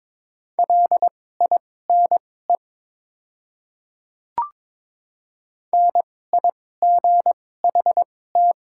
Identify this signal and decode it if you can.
Morse code, heard as a keyed tone, about 21 wpm, spaced as for 16 wpm, 700 Hz. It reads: LINE NIGHT
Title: 2x Character Spacing - Top 300 Words - 22wpm